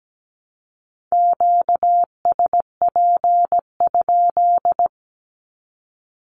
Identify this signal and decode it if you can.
Morse code, heard as a keyed tone, about 17 wpm, 700 Hz. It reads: QSP?
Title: Q-Codes 17wpm